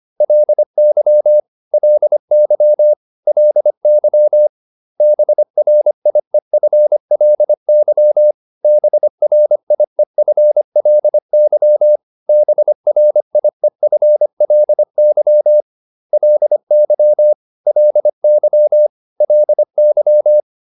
Morse code, heard as a keyed tone, about 25 wpm, 600 Hz. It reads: LY LY LY BRIEFLY BRIEFLY BRIEFLY LY LY LY